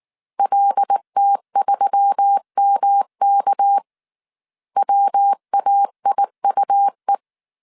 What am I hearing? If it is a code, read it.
Morse code, heard as a keyed tone, about 19 wpm, 800 Hz. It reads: LT3MK WAIUE
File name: CW / DET / 1 6 1500C S C 18wpm 2356